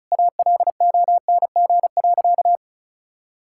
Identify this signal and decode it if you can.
Morse code, heard as a keyed tone, about 36 wpm, 700 Hz. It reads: ALONG.
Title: Sentences from Top 1600 Words 35wpm